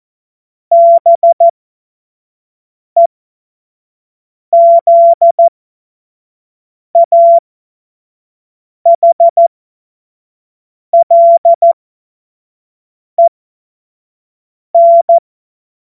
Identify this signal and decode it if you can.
Morse code, heard as a keyed tone, about 14 wpm, spaced as for 5 wpm, 680 Hz. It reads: BEZAHLEN
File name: Morsekurs / Witze / Witz 30. November 2021, 25 F70